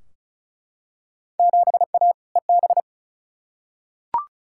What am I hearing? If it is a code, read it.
Morse code, heard as a keyed tone, about 36 wpm, 700 Hz. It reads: 7A EB
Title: Field Day Exchanges Encore - 35wpm